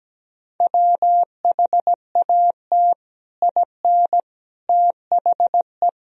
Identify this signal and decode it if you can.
Morse code, heard as a keyed tone, about 17 wpm, 700 Hz. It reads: WHAT IN THE
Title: Sentences from Top 1000 Words - Review Part 1 - 17wpm